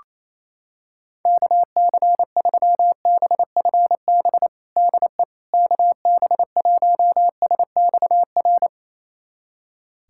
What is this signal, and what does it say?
Morse code, heard as a keyed tone, about 28 wpm, 700 Hz. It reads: KC3BFB DE KB1SXR
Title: USA Callsign DE Callsign - 6 Characters - 28wpm